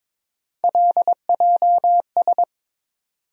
Morse code, heard as a keyed tone, about 22 wpm, 700 Hz. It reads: LJS
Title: USA Call Sign Postfixes - 3 Letters 22wpm